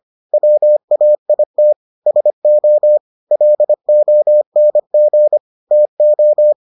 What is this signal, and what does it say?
Morse code, heard as a keyed tone, about 25 wpm, 600 Hz. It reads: WAIT SO LONG TO